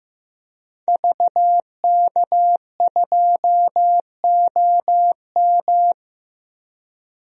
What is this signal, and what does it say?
Morse code, heard as a keyed tone, about 15 wpm, 700 Hz. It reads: VK2OM
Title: DXCC-100 Call Signs - 5 Characters - 15wpm